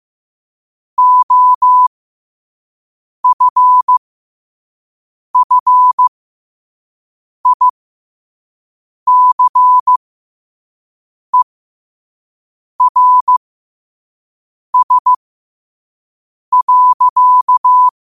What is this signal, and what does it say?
Morse code, heard as a keyed tone, about 15 wpm, spaced as for 5 wpm, 1000 Hz. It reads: OFFICERS.
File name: qotdfiles / 5wpm